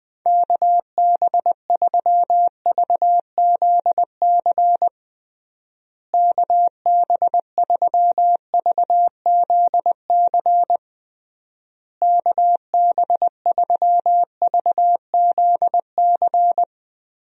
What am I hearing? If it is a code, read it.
Morse code, heard as a keyed tone, about 20 wpm, 700 Hz. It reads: KB3VZC KB3VZC KB3VZC
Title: USA Call Signs - 6 Characters - Repeated 3x - 20wpm